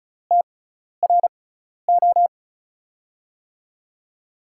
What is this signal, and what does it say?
Morse code, heard as a keyed tone, about 36 wpm, spaced as for 12 wpm, 700 Hz. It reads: TRO
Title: USA Call Signs - 6 Characters - 6x Character Spacing - 35wpm